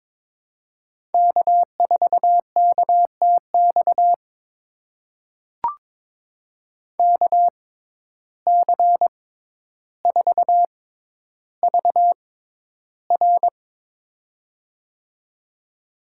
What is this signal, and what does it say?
Morse code, heard as a keyed tone, about 22 wpm, spaced as for 11 wpm, 700 Hz. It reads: K4KTX KC4VR
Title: USA Call Signs - 5 Characters - 6x Character Spacing - 22wpm